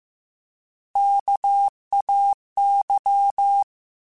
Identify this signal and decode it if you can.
Morse code, heard as a keyed tone, about 15 wpm, 800 Hz. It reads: KAY
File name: Alphabetical / Alphabetical mp3s / Ks @13